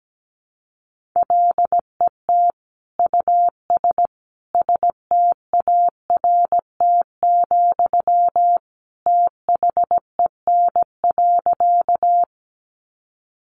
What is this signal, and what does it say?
Morse code, heard as a keyed tone, about 17 wpm, 700 Hz. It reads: LET US START, THEN.
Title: Sentences from Top 300 Words 17wpm